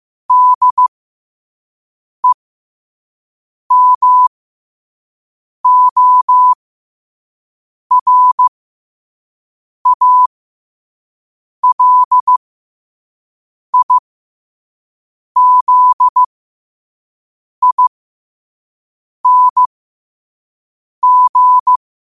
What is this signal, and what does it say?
Morse code, heard as a keyed tone, about 15 wpm, spaced as for 5 wpm, 1000 Hz. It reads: DEMORALIZING
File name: 5wpm